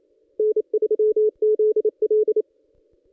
Morse code, heard as a keyed tone, about 31 wpm, 430 Hz. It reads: N3ZL
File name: contests / ss-cw-k5tr-op-n5rz -1700